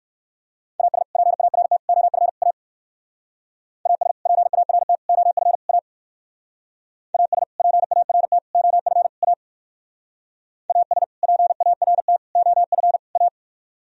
Morse code, heard as a keyed tone, about 58 wpm, 700 Hz. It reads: AS PART OF A AS PART OF A AS PART OF A AS PART OF A